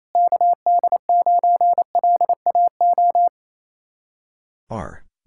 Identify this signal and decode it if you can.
Morse code, heard as a keyed tone, about 35 wpm, 700 Hz. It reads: KD9LAO